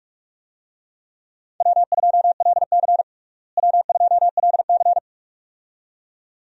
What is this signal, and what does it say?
Morse code, heard as a keyed tone, about 46 wpm, 700 Hz. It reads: W2LC W2LC
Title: USA Call Signs - 4 Characters - Repeated 2x - 45wpm